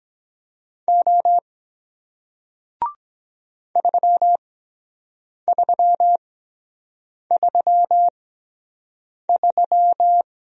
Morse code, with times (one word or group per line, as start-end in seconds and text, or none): 0.9-1.4 O
3.8-10.2 3333